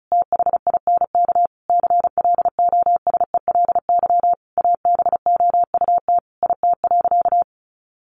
Morse code, heard as a keyed tone, about 34 wpm, 700 Hz. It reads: THINK CLOSELY ABOUT IT.
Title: Sentences from Top 1400 Words 35wpm